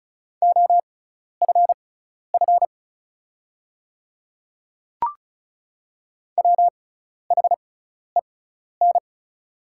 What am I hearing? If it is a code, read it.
Morse code, heard as a keyed tone, about 36 wpm, spaced as for 12 wpm, 700 Hz. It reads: OFF WHEN
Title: 6x Character Spacing - Top 300 Words - 35wpm